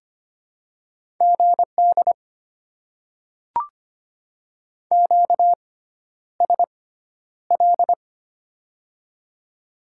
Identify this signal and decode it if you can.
Morse code, heard as a keyed tone, about 25 wpm, spaced as for 11 wpm, 690 Hz. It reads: GD QSL
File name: Top 90 QSO Elements - 6x Character Spacing - 25wpm